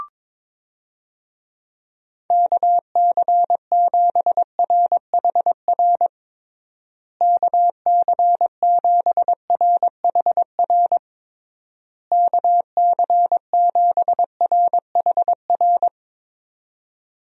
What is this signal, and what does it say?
Morse code, heard as a keyed tone, about 22 wpm, 700 Hz. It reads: KC7RHR KC7RHR KC7RHR
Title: USA Call Signs - 6 Characters - Repeated 3x - 22wpm